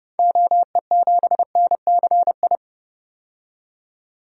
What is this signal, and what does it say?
Morse code, heard as a keyed tone, about 30 wpm, 700 Hz. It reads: OE7NCI
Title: DXCC-100 Call Signs - 6 Characters - 30wpm